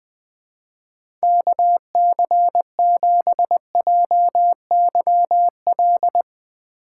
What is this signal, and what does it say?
Morse code, heard as a keyed tone, about 20 wpm, 700 Hz. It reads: KC7JYL
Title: Speed Racing - USA Call Signs 17wpm